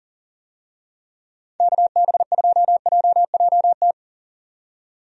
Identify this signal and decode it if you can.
Morse code, heard as a keyed tone, about 37 wpm, 700 Hz. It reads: KB2JJT
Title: USA Call Signs - 6 Characters - 40wpm